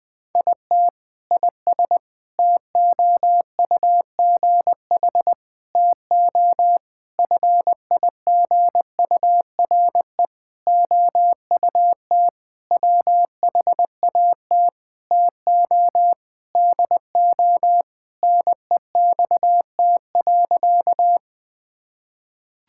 Morse code, heard as a keyed tone, about 21 wpm, 700 Hz. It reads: IT IS TOUGH TO FIGURE OUT WHAT TO DO NEXT.
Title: Sentences with 6-Word Phrases - 20wpm